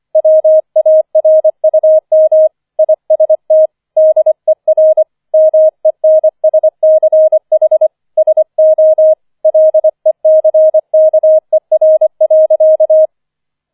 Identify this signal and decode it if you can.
Morse code, heard as a keyed tone, about 25 wpm, 630 Hz. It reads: WARUM IST DER MENSCH SO LECKER.